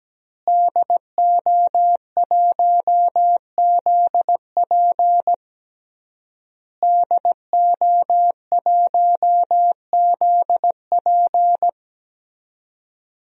Morse code, heard as a keyed tone, about 17 wpm, 700 Hz. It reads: DO1ZP DO1ZP
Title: DXCC-100 Call Signs - 5 Characters - Repeated 3x - 17wpm